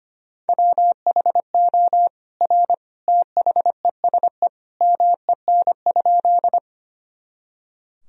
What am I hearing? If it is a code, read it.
Morse code, heard as a keyed tone, about 25 wpm, 700 Hz. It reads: WHO R THESE MEN?
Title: Sentences from Top 200 Words and Abbreviations 25wpm